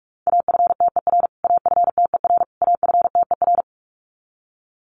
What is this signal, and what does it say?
Morse code, heard as a keyed tone, about 44 wpm, 700 Hz. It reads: AFTER AFTER AFTER
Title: Top 500 Words Repeated 45wpm